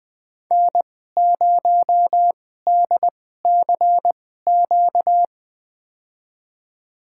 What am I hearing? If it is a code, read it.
Morse code, heard as a keyed tone, about 20 wpm, spaced as for 14 wpm, 700 Hz. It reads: N0DCQ
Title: USA Call Signs - 5 Characters - 2x Character Spacing - 20wpm